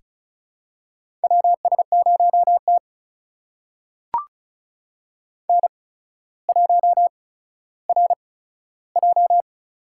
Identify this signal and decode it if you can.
Morse code, heard as a keyed tone, about 36 wpm, spaced as for 15 wpm, 700 Hz. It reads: WS0T N1RJ
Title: USA Call Signs - 4 Characters - 8x Character Spacing - 35wpm